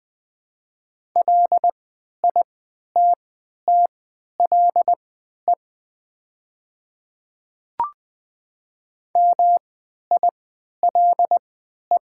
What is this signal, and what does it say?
Morse code, heard as a keyed tone, about 20 wpm, spaced as for 11 wpm, 700 Hz. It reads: LITTLE MILE
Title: 3x Character Spacing - Top 300 Words - 20wpm